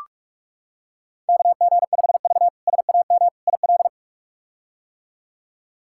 Morse code, heard as a keyed tone, about 46 wpm, 700 Hz. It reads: KG5V SAM IL
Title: K1USN SST QSO Practice - 45wpm